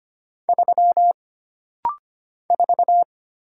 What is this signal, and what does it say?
Morse code, heard as a keyed tone, about 25 wpm, 700 Hz. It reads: 3 4
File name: Single Character (Letter:Number:Punctuation:Prosign) - Rapid-Fire - 25wpm